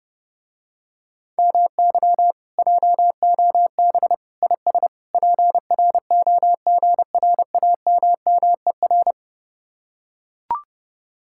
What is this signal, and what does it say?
Morse code, heard as a keyed tone, about 31 wpm, 700 Hz. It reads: MY JOB IS PROGRAMMER